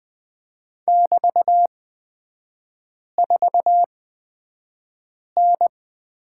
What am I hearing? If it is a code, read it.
Morse code, heard as a keyed tone, about 20 wpm, spaced as for 5 wpm, 690 Hz. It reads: <BT>4N
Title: 229 Lesson 37e - Letter x - Three Characters 20wpm